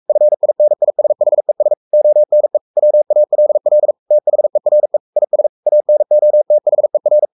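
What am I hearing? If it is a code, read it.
Morse code, heard as a keyed tone, about 44 wpm, 600 Hz. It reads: FINISHES ONE WALL THERE IS ANOTHER
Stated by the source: Morse code at 43 wpm sending Ken Follett's novel Never in English, from CW med Gnister.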